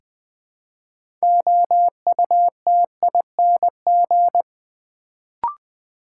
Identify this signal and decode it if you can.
Morse code, heard as a keyed tone, about 20 wpm, 700 Hz. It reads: OUTING
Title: Build A Word - 2 Letters At A Time - 6-Letter Words - 20wpm